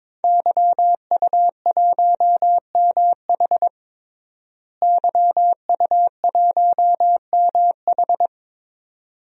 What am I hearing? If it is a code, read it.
Morse code, heard as a keyed tone, about 22 wpm, 700 Hz. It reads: YU1MH YU1MH
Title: DXCC-100 Call Signs - 5 Characters - Repeated 3x - 22wpm